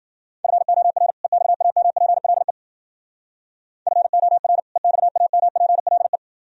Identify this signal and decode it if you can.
Morse code, heard as a keyed tone, about 49 wpm, 700 Hz. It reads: FOR EXAMPLE FOR EXAMPLE
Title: Speed-Racing - Sets of 2 Words 40wpm